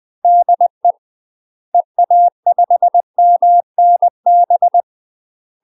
Morse code, heard as a keyed tone, about 20 wpm, 700 Hz. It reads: DE EA5MNB
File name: UK MASTER QSOs TRAINING / 20WPM / UK MASTER QSOs 0023